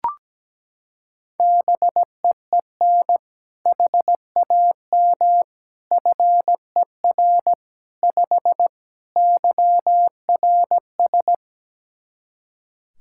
Morse code, heard as a keyed tone, about 17 wpm, 700 Hz. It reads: BEEN HAM FER 5 YRS
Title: K1USN SST QSO Practice 17wpm